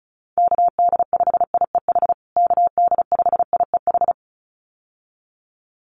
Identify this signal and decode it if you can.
Morse code, heard as a keyed tone, about 34 wpm, 700 Hz. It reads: KD5IEH KD5IEH
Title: USA Call Signs Repeated 2x 35wpm